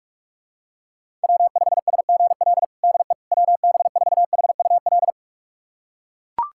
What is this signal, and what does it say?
Morse code, heard as a keyed tone, about 41 wpm, 700 Hz. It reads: W5SGL DE WB4HUL
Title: USA Callsign DE Callsign 45wpm